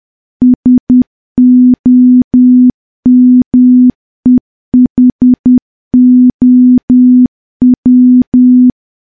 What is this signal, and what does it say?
Morse code, heard as a keyed tone, about 10 wpm, 250 Hz. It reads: SOMEHOW
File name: morse / morse-training-10WPM